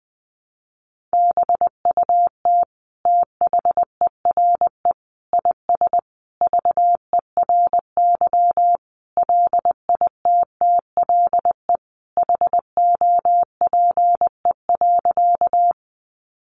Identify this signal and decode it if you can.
Morse code, heard as a keyed tone, about 20 wpm, 700 Hz. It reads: BUT THERE IS VERY LITTLE HOPE.
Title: Sentences from Top 1000 Words 20wpm